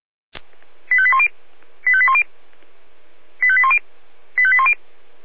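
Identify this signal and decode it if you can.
Five-tone selective call (ZVEI), heard as quick runs of five tones, tones of about 70 ms. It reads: 85810 85810 85810 85810
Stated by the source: Received on an SDR, ZVEI five-tone selective call, channel 496.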